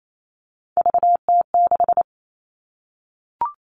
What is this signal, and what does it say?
Morse code, heard as a keyed tone, about 28 wpm, 700 Hz. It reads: VT6
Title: All Cut Numbers - 3-Digit 28wpm